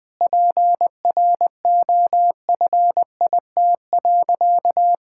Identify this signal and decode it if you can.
Morse code, heard as a keyed tone, about 20 wpm, 700 Hz. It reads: PROFIT.
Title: Sentences from Top 1600 Words 20wpm